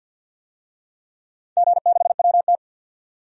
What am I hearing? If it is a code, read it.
Morse code, heard as a keyed tone, about 47 wpm, 700 Hz. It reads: G6WT
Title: DXCC-100 Call Signs - 4 Characters - 50wpm